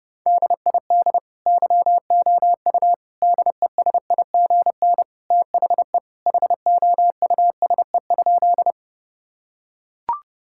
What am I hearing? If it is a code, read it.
Morse code, heard as a keyed tone, about 30 wpm, 700 Hz. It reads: DID YOU DESIGN THE HOUSE?
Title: Sentences from Top 1000 Words - Review Part 5 - 30wpm